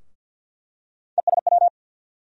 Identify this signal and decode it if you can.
Morse code, heard as a keyed tone, about 47 wpm, 700 Hz. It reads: ESW
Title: USA Call Sign Postfixes - 3 Letters 50wpm